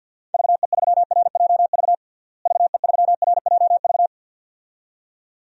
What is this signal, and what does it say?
Morse code, heard as a keyed tone, about 50 wpm, 700 Hz. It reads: VE3RJV VE3RJV
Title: DXCC-100 Call Signs - 6 Characters - Repeated 2x - 50wpm